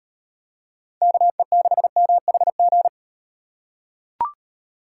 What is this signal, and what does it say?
Morse code, heard as a keyed tone, about 38 wpm, 700 Hz. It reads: KE6MHG